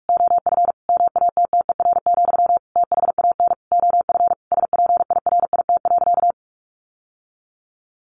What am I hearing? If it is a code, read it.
Morse code, heard as a keyed tone, about 44 wpm, 700 Hz. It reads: OF MATTER, THAN OF SPIRIT.
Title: Sentences from Top 1400 Words 45wpm